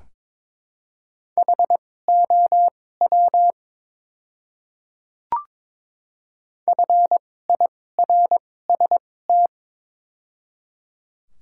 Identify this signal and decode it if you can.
Morse code, heard as a keyed tone, about 22 wpm, spaced as for 16 wpm, 700 Hz. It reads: HOW FIRST
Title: 2x Character Spacing - Top 100 Words - 22wpm